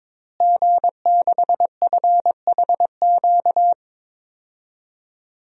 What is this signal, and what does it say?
Morse code, heard as a keyed tone, about 22 wpm, 700 Hz. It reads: G6FHQ
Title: DXCC-100 Call Signs - 5 Characters - Repeated 3x - 22wpm